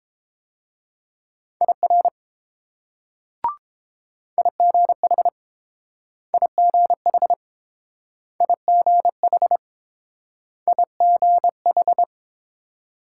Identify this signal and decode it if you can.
Morse code, heard as a keyed tone, about 30 wpm, 700 Hz. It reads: IR IGH IGH IGH IGH